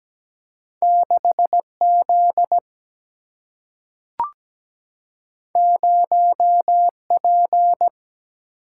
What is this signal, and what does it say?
Morse code, heard as a keyed tone, about 17 wpm, 700 Hz. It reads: 6 Z 0 P
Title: Single-Letter Single-Number 17wpm